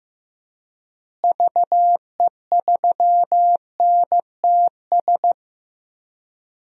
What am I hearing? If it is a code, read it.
Morse code, heard as a keyed tone, about 15 wpm, 700 Hz. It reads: VE3NTS
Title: DXCC-100 Call Signs - 6 Characters - Repeated 3x - 15wpm